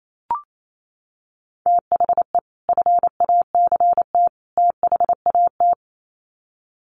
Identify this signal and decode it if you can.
Morse code, heard as a keyed tone, about 28 wpm, 700 Hz. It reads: THE FACT THAT
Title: Sets of 3 Words 28wpm